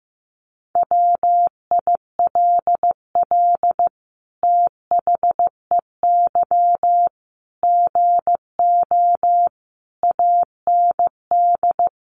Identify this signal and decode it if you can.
Morse code, heard as a keyed tone, about 15 wpm, 700 Hz. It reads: WILL THEY GO AND
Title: Sentences from Top 1100 Words 15wpm